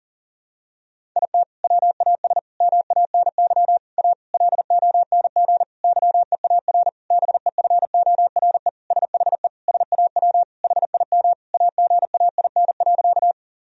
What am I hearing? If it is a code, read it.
Morse code, heard as a keyed tone, about 40 wpm, 700 Hz. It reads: IT WAS MANY A LONG YEAR BEFORE SHE SAW HIM AGAIN.
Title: Sentences from Top 1000 Words - Review Part 1 - 40wpm